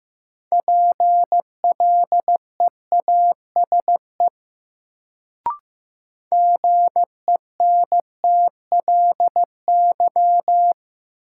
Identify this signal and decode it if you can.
Morse code, heard as a keyed tone, about 15 wpm, 700 Hz. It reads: PLEASE GENTLY
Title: Sentences from Top 1500 Words - New Word Drill - 15wpm